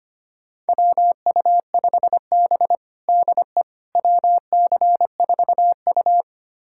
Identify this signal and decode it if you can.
Morse code, heard as a keyed tone, about 25 wpm, 700 Hz. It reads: WU5B DE WC4U
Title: USA Callsign DE Callsign - 4 Characters - 25wpm